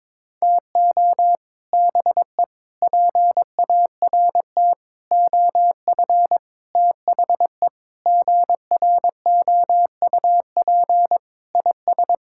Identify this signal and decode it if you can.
Morse code, heard as a keyed tone, about 22 wpm, 700 Hz. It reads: TO BE PART OF THE GROUP IS
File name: practice / 22wpm / Sentences with 4-Word Phrases - 22wpm